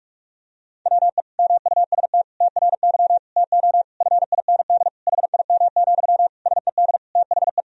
Morse code, heard as a keyed tone, about 46 wpm, 700 Hz. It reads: WE MUST TRY TO FIND HIM, SED THE
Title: Sentences from Top 400 Words and Abbreviations 45wpm